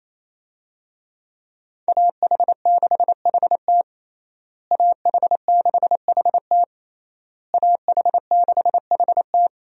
Morse code, heard as a keyed tone, about 28 wpm, 700 Hz. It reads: AH6HT AH6HT AH6HT
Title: USA Call Signs - 5 Characters - Repeated 3x - 28wpm